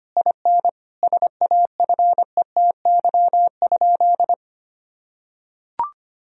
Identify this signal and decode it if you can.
Morse code, heard as a keyed tone, about 25 wpm, 700 Hz. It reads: IN SAFETY?
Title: Sentences from Top 1000 Words - Review Part 4 - 25wpm